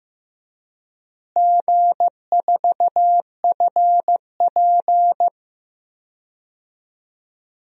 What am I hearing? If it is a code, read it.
Morse code, heard as a keyed tone, about 15 wpm, 700 Hz. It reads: G4FP
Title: DXCC-100 Call Signs - 4 Characters - Repeated 3x - 15wpm